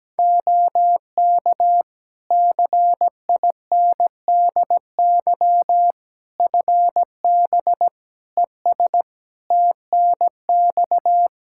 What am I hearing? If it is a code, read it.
Morse code, heard as a keyed tone, about 17 wpm, 700 Hz. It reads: OK CINDY FB ES TNX
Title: K1USN SST QSO Practice 17wpm